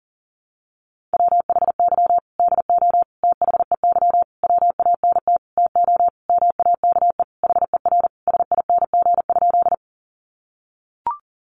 Morse code, heard as a keyed tone, about 40 wpm, 700 Hz. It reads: WHY DO THEY WANT TO MAKE HER SING?
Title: Sentences from Top 400 Words 40wpm